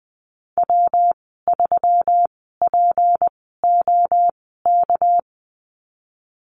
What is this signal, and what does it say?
Morse code, heard as a keyed tone, about 20 wpm, spaced as for 15 wpm, 700 Hz. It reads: W3POK
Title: 207 Lesson 33g - Letter j - Call Signs 20wpm